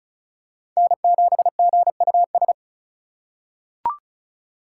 Morse code, heard as a keyed tone, about 36 wpm, 700 Hz. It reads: N7GUS
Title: USA Call Signs - 5 Characters - 3x Character Spacing - 35wpm